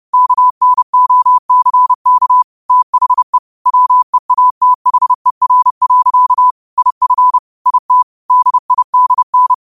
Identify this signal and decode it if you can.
Morse code, heard as a keyed tone, about 30 wpm, 1000 Hz. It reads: KNOCK THE WEATHER. IF IT DIDN
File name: qotdfiles / 30wpm